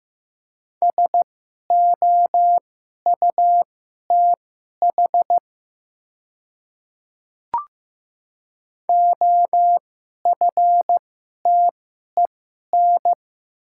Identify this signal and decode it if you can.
Morse code, heard as a keyed tone, about 15 wpm, spaced as for 11 wpm, 700 Hz. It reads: SOUTH OFTEN
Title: 2x Character Spacing - Top 500 Words - 15wpm